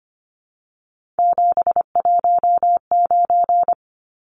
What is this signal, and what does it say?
Morse code, heard as a keyed tone, about 25 wpm, 690 Hz. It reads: 719
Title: Numbers - 3-Digit 25wpm